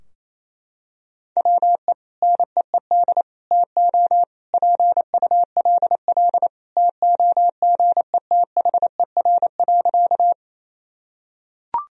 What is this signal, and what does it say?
Morse code, heard as a keyed tone, about 28 wpm, 700 Hz. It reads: WE NEED TO PULL TOGETHER.
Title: Sentences from Top 1000 Words - Review Part 2 - 28wpm